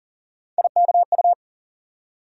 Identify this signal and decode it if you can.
Morse code, heard as a keyed tone, about 40 wpm, 700 Hz. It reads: IKU
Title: USA Call Sign Postfixes - 3 Letters 40wpm